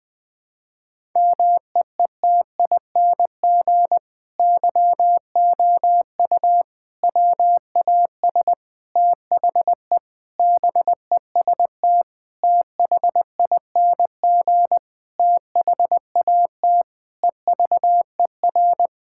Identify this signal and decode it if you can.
Morse code, heard as a keyed tone, about 20 wpm, 700 Hz. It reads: MEETING YOU WAS THE BEST THING THAT EVER
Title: Sentences with 8-Word Phrases - Spoken Beginning - 20wpm